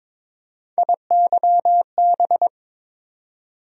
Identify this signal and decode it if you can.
Morse code, heard as a keyed tone, about 21 wpm, 700 Hz. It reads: IYB
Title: USA Call Sign Postfixes - 3 Letters 22wpm